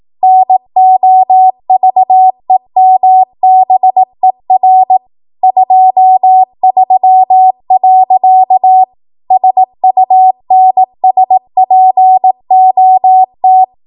Morse code, heard as a keyed tone, about 18 wpm, 750 Hz. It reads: NOVEMBER 23. SUNSPOT
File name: Archive / 18 WPM / 18WPM